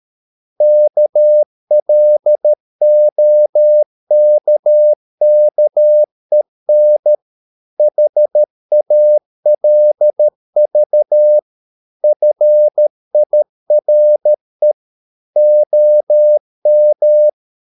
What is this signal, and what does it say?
Morse code, heard as a keyed tone, about 13 wpm, 600 Hz. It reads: KLOKKEN HALV FIRE OM
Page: Ferie 13 wpm